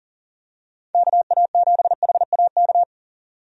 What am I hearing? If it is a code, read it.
Morse code, heard as a keyed tone, about 40 wpm, 700 Hz. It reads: KA7HAK
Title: USA Call Signs - 6 Characters - 40wpm